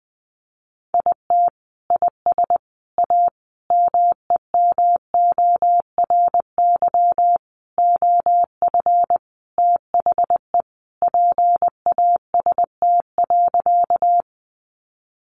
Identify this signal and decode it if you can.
Morse code, heard as a keyed tone, about 20 wpm, 700 Hz. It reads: IT IS A MEMORY OF THE PAST.
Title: Sentences from Top 1300 Words 20wpm